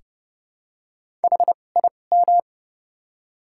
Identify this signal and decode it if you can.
Morse code, heard as a keyed tone, about 30 wpm, spaced as for 22 wpm, 700 Hz. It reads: HIM
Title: 2x Character Spacing - Top 300 Words - 30wpm